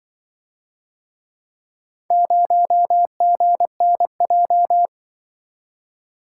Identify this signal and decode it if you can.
Morse code, heard as a keyed tone, about 23 wpm, 700 Hz. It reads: 0GNJ
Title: Speed-Racing - Four Letter-Number Combinations - 20wpm